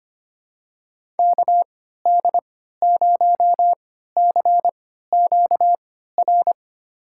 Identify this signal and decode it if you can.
Morse code, heard as a keyed tone, about 25 wpm, spaced as for 14 wpm, 700 Hz. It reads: KD0CQR
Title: USA Call Signs - 6 Characters - 3x Character Spacing - 25wpm